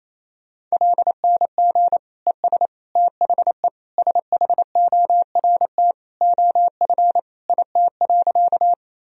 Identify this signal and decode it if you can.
Morse code, heard as a keyed tone, about 28 wpm, 700 Hz. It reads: LNG ES THE SHORT OF IT.
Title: Sentences from Top 400 Words and Abbreviations 28wpm